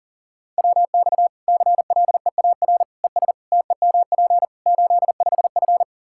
Morse code, heard as a keyed tone, about 40 wpm, 700 Hz. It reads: WX CLEAR ES TEMP 85F
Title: Practice QSO - Exchange 2 - Info - 40wpm